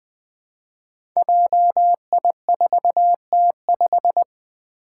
Morse code, heard as a keyed tone, about 20 wpm, 700 Hz. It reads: JI4T5
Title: Five Challenging Letter-Number Combinations 20wpm